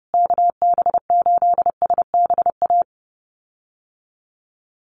Morse code, heard as a keyed tone, about 30 wpm, 700 Hz. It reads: KB8SBA
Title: USA Call Signs Repeated 3x 30wpm